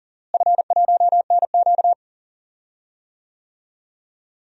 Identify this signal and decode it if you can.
Morse code, heard as a keyed tone, about 37 wpm, 700 Hz. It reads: F1NQ
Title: DXCC-100 Call Signs - 4 Characters - Repeated 2x - 40wpm